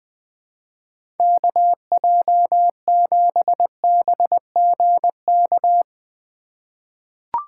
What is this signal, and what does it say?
Morse code, heard as a keyed tone, about 21 wpm, 700 Hz. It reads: KJ7BGK
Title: USA Call Signs - 6 Characters - 2x Character Spacing - 20wpm